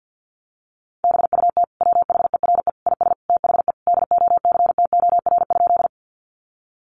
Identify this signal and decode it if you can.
Morse code, heard as a keyed tone, about 50 wpm, 700 Hz. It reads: BUT WHERE IS THE DOCTOR?
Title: Sentences from Top 1500 Words 50wpm